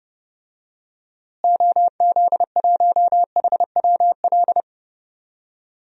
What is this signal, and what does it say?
Morse code, heard as a keyed tone, about 31 wpm, 700 Hz. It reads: OZ1HWL
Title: DXCC-100 Call Signs - 6 Characters - Repeated 3x - 30wpm